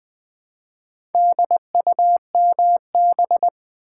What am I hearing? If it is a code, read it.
Morse code, heard as a keyed tone, about 20 wpm, 700 Hz. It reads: DUMB
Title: Build A Word - 2 Letters At A Time - 4-Letter Words - 20wpm